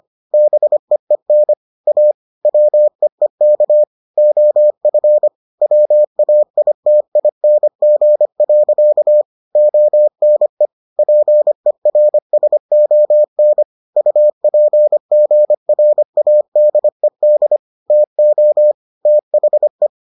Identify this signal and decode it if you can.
Morse code, heard as a keyed tone, about 25 wpm, 600 Hz. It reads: BEEN A WEEK OF WAITING. ONE PERSON UPGRADED TO THE